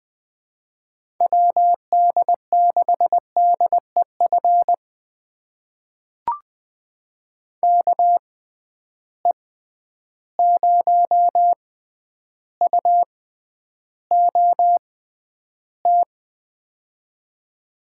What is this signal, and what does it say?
Morse code, heard as a keyed tone, about 21 wpm, spaced as for 10 wpm, 700 Hz. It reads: WD6DEF KE0UOT
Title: USA Call Signs - 6 Characters - 6x Character Spacing - 20wpm